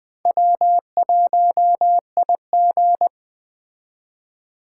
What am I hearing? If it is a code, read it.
Morse code, heard as a keyed tone, about 20 wpm, 700 Hz. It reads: W1IG
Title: Speed Racing - USA 1x2 Call Signs 17wpm